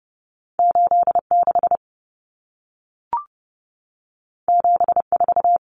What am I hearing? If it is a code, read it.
Morse code, heard as a keyed tone, about 30 wpm, 700 Hz. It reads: 86 74
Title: Numbers - 2-Digit 30wpm